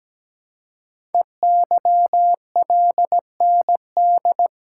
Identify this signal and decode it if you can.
Morse code, heard as a keyed tone, about 17 wpm, 700 Hz. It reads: EYLND
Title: Code Groups - 5 Letters - 17wpm